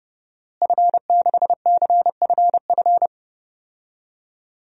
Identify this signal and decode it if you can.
Morse code, heard as a keyed tone, about 30 wpm, 700 Hz. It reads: F6CFF
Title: DXCC-100 Call Signs - 5 Characters - Repeated 3x - 30wpm